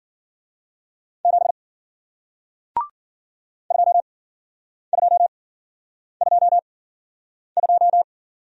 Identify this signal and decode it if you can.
Morse code, heard as a keyed tone, about 53 wpm, spaced as for 9 wpm, 700 Hz. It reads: 7 2222